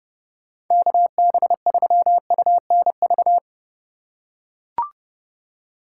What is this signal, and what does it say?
Morse code, heard as a keyed tone, about 32 wpm, 700 Hz. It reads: KB3UNV
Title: USA Call Signs - 6 Characters - 3x Character Spacing - 30wpm